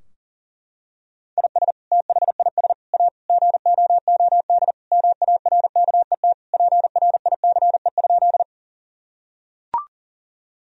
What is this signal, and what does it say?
Morse code, heard as a keyed tone, about 40 wpm, 700 Hz. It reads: IS THIS A GOOD MARKET PRICE?
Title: Sentences from Top 1000 Words - Review Part 3 - 40wpm